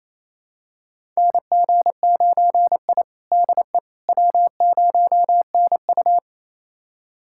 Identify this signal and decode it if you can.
Morse code, heard as a keyed tone, about 28 wpm, 700 Hz. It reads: NG9I DE W0NU
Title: USA Callsign DE Callsign - 4 Characters - 28wpm